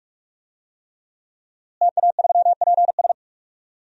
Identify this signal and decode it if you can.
Morse code, heard as a keyed tone, about 41 wpm, 700 Hz. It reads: TA3PS
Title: DXCC-100 Call Signs - 5 Characters - Repeated 3x - 45wpm